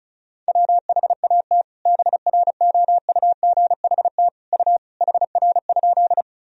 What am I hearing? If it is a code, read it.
Morse code, heard as a keyed tone, about 36 wpm, 700 Hz. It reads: WHAT BROUGHT U HR?
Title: Sentences from Top 500 Words and Abbreviations 35wpm